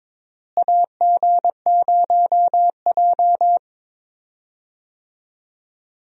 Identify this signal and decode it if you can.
Morse code, heard as a keyed tone, about 22 wpm, 700 Hz. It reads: AG0J
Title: USA Call Signs - 4 Characters - Repeated 2x - 22wpm